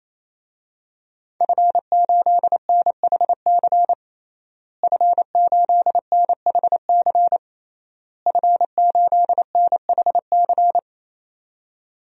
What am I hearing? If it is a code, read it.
Morse code, heard as a keyed tone, about 28 wpm, 700 Hz. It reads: F8NHC F8NHC F8NHC
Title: DXCC-100 Call Signs - 5 Characters - Repeated 3x - 28wpm